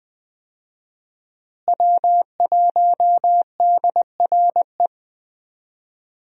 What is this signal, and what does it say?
Morse code, heard as a keyed tone, about 20 wpm, 700 Hz. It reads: W1DRE
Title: USA Call Signs - 5 Characters - Repeated 2x - 20wpm